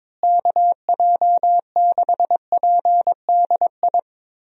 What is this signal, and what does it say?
Morse code, heard as a keyed tone, about 22 wpm, 700 Hz. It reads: KJ6PDI
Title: USA Call Signs - 6 Characters - Repeated 3x - 22wpm